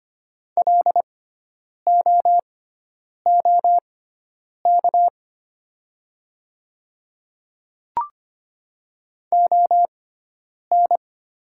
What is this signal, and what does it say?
Morse code, heard as a keyed tone, about 25 wpm, spaced as for 9 wpm, 700 Hz. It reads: LOOK ON
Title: 6x Character Spacing - Top 100 Words - 25wpm